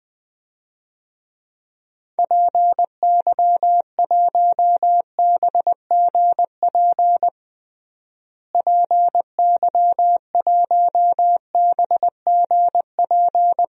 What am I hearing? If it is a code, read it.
Morse code, heard as a keyed tone, about 20 wpm, 700 Hz. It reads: PY1BGP PY1BGP
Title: DXCC-100 Call Signs - 6 Characters - Repeated 3x - 20wpm